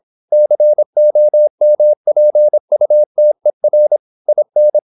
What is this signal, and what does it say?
Morse code, heard as a keyed tone, about 26 wpm, 600 Hz. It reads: COMPUTER IN
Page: Never 26wpm | CW med Gnister